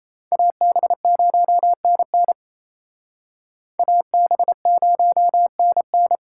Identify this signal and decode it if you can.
Morse code, heard as a keyed tone, about 35 wpm, 700 Hz. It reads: AB0NN AB0NN
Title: Speed Racing - USA 2x2 Call Signs 28wpm